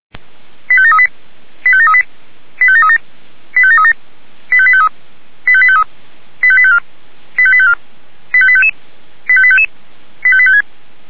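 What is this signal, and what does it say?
Five-tone selective call (ZVEI), heard as quick runs of five tones, tones of about 70 ms. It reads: 85738 85738 85737 85737 85853 85853 85854 85854 85855 85855 85856
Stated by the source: Five-tone ZVEI selective call, SDR receiver, 464 channel.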